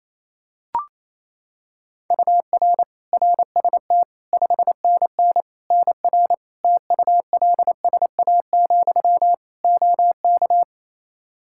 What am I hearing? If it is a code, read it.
Morse code, heard as a keyed tone, about 28 wpm, 700 Hz. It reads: UR RST 5NN NR TULSA, OK
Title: Practice QSO - Exchange 1 - Beginning - 28wpm